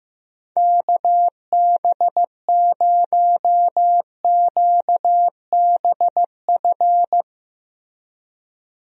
Morse code, heard as a keyed tone, about 15 wpm, 700 Hz. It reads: KB0QBF
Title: USA Call Signs - 6 Characters - Repeated 2x - 15wpm